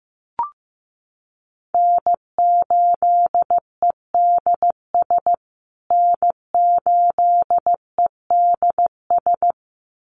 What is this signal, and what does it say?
Morse code, heard as a keyed tone, about 15 wpm, 700 Hz. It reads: N8EDS N8EDS
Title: USA Call Signs Repeated 2x 15wpm